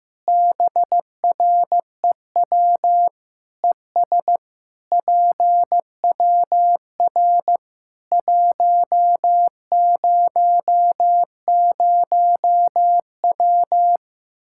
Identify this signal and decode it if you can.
Morse code, heard as a keyed tone, about 15 wpm, 700 Hz. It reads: BREW ES PWR 100W
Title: Practice QSO - Exchange 2 - Info - 15wpm